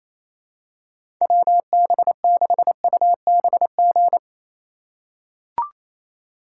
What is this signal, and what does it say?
Morse code, heard as a keyed tone, about 28 wpm, 700 Hz. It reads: WB6UBG
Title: USA Call Signs - 6 Characters - 3x Character Spacing - 28wpm